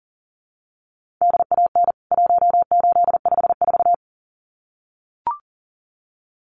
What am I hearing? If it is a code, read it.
Morse code, heard as a keyed tone, about 37 wpm, 700 Hz. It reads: DAN 1854
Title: CWT Exchange 40wpm